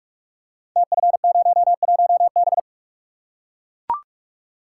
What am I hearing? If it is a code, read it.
Morse code, heard as a keyed tone, about 44 wpm, 690 Hz. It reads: TF01B
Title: Code Groups - 5 Letter:Numbers - 45wpm